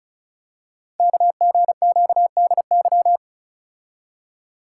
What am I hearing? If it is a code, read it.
Morse code, heard as a keyed tone, about 36 wpm, 700 Hz. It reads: KGQDY
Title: Code Groups - 5 Letters - 35wpm